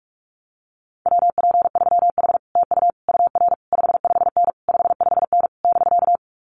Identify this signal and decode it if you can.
Morse code, heard as a keyed tone, about 46 wpm, 700 Hz. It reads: WP3H TU UR 55N 55N <BK>
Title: Common QSO Phrases Encore 45wpm